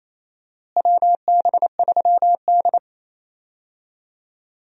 Morse code, heard as a keyed tone, about 28 wpm, 700 Hz. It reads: WB3D
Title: USA Call Signs - 4 Characters - Repeated 3x - 28wpm